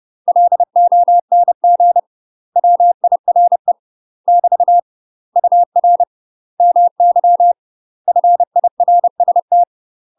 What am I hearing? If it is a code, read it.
Morse code, heard as a keyed tone, about 30 wpm, 700 Hz. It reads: LONG WIRE <BT> UR MY FIRST
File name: UK MASTER QSOs TRAINING / 30WPM / UK MASTER QSOs 0004